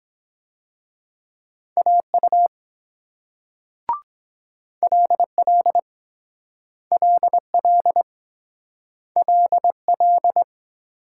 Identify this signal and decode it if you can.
Morse code, heard as a keyed tone, about 23 wpm, 700 Hz. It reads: AU LL LL LL